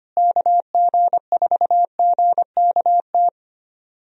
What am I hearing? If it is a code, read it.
Morse code, heard as a keyed tone, about 25 wpm, 700 Hz. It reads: KG4GKT
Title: USA Call Signs - 6 Characters - Repeated 2x - 25wpm